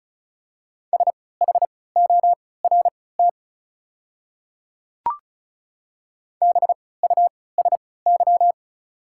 Morse code, heard as a keyed tone, about 34 wpm, spaced as for 20 wpm, 700 Hz. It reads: SHORT BUSY
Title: 3x Character Spacing - Top 500 Words - 35wpm